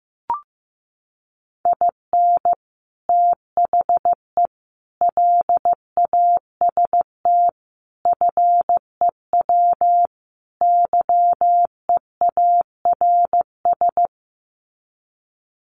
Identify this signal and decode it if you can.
Morse code, heard as a keyed tone, about 14 wpm, 700 Hz. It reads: IN THE LAST FEW YEARS
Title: Sets of 5 Words 15wpm